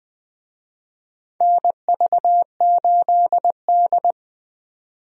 Morse code, heard as a keyed tone, about 20 wpm, 700 Hz. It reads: NV8D
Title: USA Call Signs - 4 Characters - 20wpm